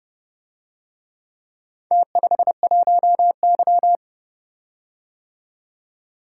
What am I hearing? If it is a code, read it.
Morse code, heard as a keyed tone, about 29 wpm, 700 Hz. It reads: T51Y